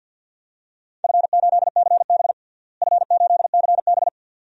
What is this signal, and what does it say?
Morse code, heard as a keyed tone, about 52 wpm, 700 Hz. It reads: F8CB F8CB
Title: DXCC-100 Call Signs - 4 Characters - Repeated 2x - 50wpm